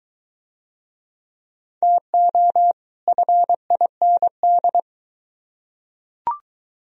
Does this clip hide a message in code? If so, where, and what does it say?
Yes: Morse code from 1.8 to 4.8 s, heard as a keyed tone, about 22 wpm, 700 Hz. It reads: TO FIND